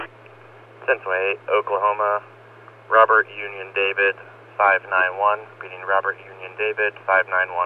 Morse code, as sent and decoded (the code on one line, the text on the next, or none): none